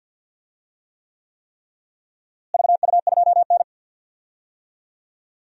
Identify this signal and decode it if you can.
Morse code, heard as a keyed tone, about 50 wpm, 700 Hz. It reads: VU2N
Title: DXCC-100 Call Signs - 4 Characters - Repeated 3x - 50wpm